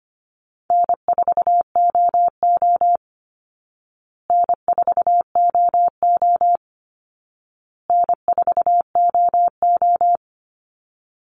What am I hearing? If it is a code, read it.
Morse code, heard as a keyed tone, about 25 wpm, 700 Hz. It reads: N4OO N4OO N4OO
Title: Easy Call Signs Repeated 3x - 25wpm